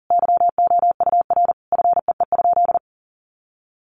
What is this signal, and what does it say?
Morse code, heard as a keyed tone, about 37 wpm, 700 Hz. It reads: YOUR FEE?
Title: Sentences from Top 1500 Words 40wpm